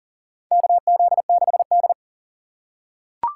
Morse code, heard as a keyed tone, about 40 wpm, 700 Hz. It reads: KZ6D
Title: USA Call Signs - 4 Characters - 6x Character Spacing - 40wpm